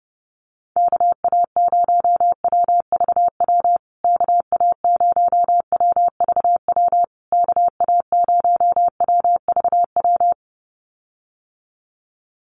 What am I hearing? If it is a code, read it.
Morse code, heard as a keyed tone, about 30 wpm, 700 Hz. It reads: KA0WVW KA0WVW KA0WVW
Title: USA Call Signs Repeated 3x 30wpm